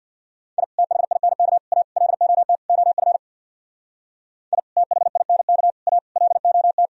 Morse code, heard as a keyed tone, about 55 wpm, 700 Hz. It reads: I THINK A LOT OF I THINK A LOT
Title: Speed-Racing - Sets of 5 Words 50wpm